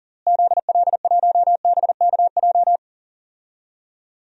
Morse code, heard as a keyed tone, about 40 wpm, 700 Hz. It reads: ZL1BKJ
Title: DXCC-100 Call Signs - 6 Characters - Repeated 2x - 40wpm